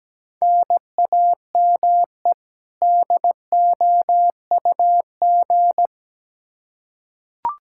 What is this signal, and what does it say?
Morse code, heard as a keyed tone, about 17 wpm, 700 Hz. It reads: NAME DOUG
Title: Practice QSO - Exchange 1 - Beginning - 17wpm